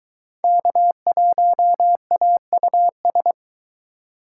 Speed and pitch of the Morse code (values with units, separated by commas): 23 wpm, 700 Hz